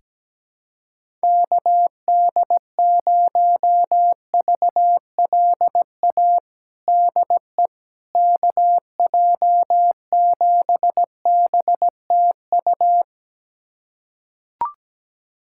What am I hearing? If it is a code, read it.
Morse code, heard as a keyed tone, about 17 wpm, 700 Hz. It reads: KD0VLA DE KJ7BTU
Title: USA Callsign DE Callsign - 6 Characters - 17wpm